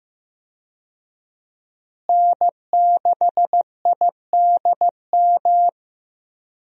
Morse code, heard as a keyed tone, about 15 wpm, 700 Hz. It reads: N6IDM
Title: USA Call Signs - 5 Characters - Repeated 3x - 15wpm